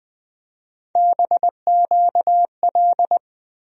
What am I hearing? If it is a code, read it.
Morse code, heard as a keyed tone, about 20 wpm, 700 Hz. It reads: BQL
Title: USA Call Sign Postfixes - 3 Letters 20wpm